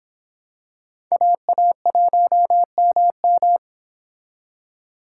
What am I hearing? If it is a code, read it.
Morse code, heard as a keyed tone, about 24 wpm, 700 Hz. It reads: AA1MM